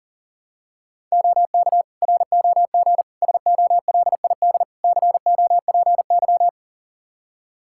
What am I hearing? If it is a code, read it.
Morse code, heard as a keyed tone, about 40 wpm, 700 Hz. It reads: OK ROG SOLID COPY
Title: Practice QSO - Exchange 3 - More Info - 40wpm